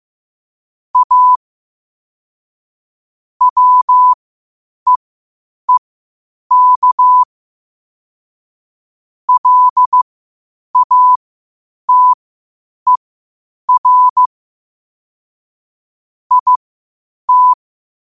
Morse code, heard as a keyed tone, about 15 wpm, spaced as for 8 wpm, 1000 Hz. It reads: A WEEK LATER IT
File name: qotdfiles / 8wpm